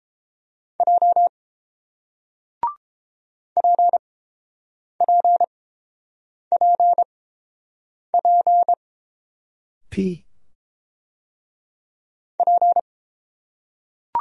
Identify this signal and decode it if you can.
Morse code, heard as a keyed tone, about 30 wpm, spaced as for 7 wpm, 700 Hz. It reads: J PPPP P